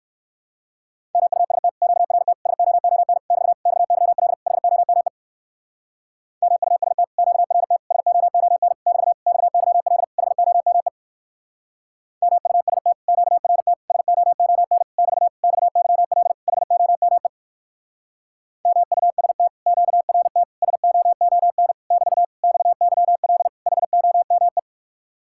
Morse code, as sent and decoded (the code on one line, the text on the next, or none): -- ..- ... - / --.- .-. - / ... --- --- -. / -...- / -..- -.-- .-.. / .... --- -- . / -- ..- ... - / --.- .-. - / ... --- --- -. / -...- / -..- -.-- .-.. / .... --- -- . / -- ..- ... - / --.- .-. - / ... --- --- -. / -...- / -..- -.-- .-.. / .... --- -- . / -- ..- ... - / --.- .-. - / ... --- --- -. / -...- / -..- -.-- .-.. / .... --- -- .
MUST QRT SOON <BT> XYL HOME MUST QRT SOON <BT> XYL HOME MUST QRT SOON <BT> XYL HOME MUST QRT SOON <BT> XYL HOME